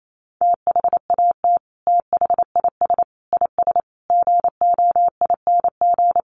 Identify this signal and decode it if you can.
Morse code, heard as a keyed tone, about 28 wpm, 700 Hz. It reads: THAT THIS IS GOING
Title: Sets of 5 Words Encore2 28wpm